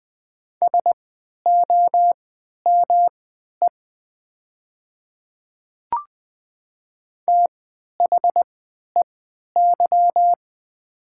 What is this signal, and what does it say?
Morse code, heard as a keyed tone, about 20 wpm, spaced as for 11 wpm, 700 Hz. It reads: SOME THEY
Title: 3x Character Spacing - Top 100 Words - 20wpm